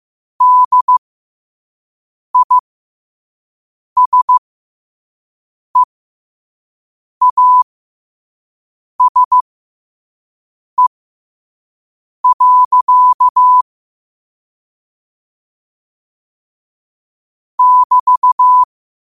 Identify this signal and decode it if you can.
Morse code, heard as a keyed tone, about 15 wpm, spaced as for 5 wpm, 1000 Hz. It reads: DISEASE. <BT>
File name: qotdfiles / 5wpm